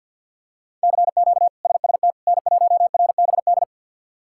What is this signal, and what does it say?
Morse code, heard as a keyed tone, about 51 wpm, 700 Hz. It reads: CQ SST N1RBD
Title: K1USN SST QSO Practice - 50wpm